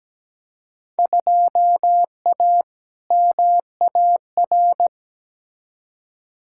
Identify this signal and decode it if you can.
Morse code, heard as a keyed tone, about 17 wpm, 700 Hz. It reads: 2A MAR
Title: Field Day Exchanges - 17wpm